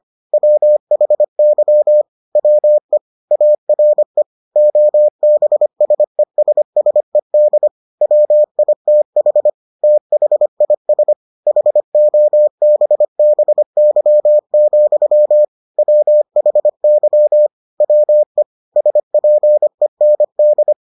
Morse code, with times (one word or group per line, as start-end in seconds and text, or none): 0.3-2.0 WHY
2.4-3.0 WE
3.3-4.2 ARE
4.6-7.7 OBSESSED
8.0-9.5 WITH
9.8-11.1 THIS
11.5-15.4 HOBBY,
15.8-17.5 WHY
17.8-18.4 WE
18.8-20.7 SPEND